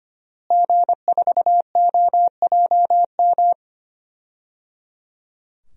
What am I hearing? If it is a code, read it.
Morse code, heard as a keyed tone, about 25 wpm, 700 Hz. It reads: G4OJM
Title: DXCC-100 Call Signs - 5 Characters - Repeated 3x - 25wpm